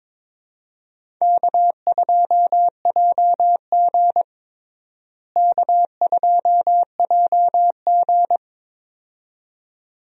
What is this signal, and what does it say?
Morse code, heard as a keyed tone, about 22 wpm, 700 Hz. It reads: K2JG K2JG
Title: USA Call Signs - 4 Characters - Repeated 2x - 22wpm